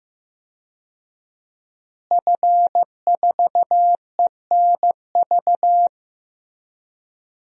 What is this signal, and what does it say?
Morse code, heard as a keyed tone, about 16 wpm, 700 Hz. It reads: F4ENV
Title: DXCC-100 Call Signs - 5 Characters - Repeated 2x - 15wpm